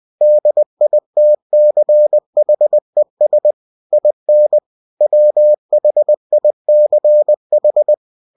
Morse code, heard as a keyed tone, about 20 wpm, 600 Hz. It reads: DITCHES IN WHICH
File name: cw-books / winnie-the-pooh / winnie-20wpm-page-0091